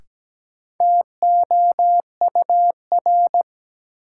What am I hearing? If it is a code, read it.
Morse code, heard as a keyed tone, about 17 wpm, 700 Hz. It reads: TOUR
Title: Sentences from Top 1400 Words - New Word Drill Alt - 17wpm